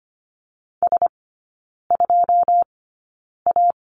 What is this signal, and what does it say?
Morse code, heard as a keyed tone, about 25 wpm, spaced as for 9 wpm, 690 Hz. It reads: S2A
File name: Lesson 14e - Number 2 - Three Characters 25wpm